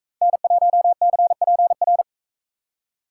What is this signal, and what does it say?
Morse code, heard as a keyed tone, about 44 wpm, 700 Hz. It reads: N1CPR